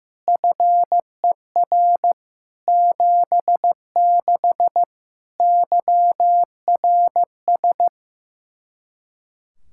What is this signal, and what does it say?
Morse code, heard as a keyed tone, about 16 wpm, 700 Hz. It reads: FER 76 YRS
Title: K1USN SST QSO Practice 15wpm